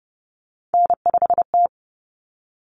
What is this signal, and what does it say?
Morse code, heard as a keyed tone, about 30 wpm, 700 Hz. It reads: N5T
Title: Common Cut Numbers - 3-Digit 30wpm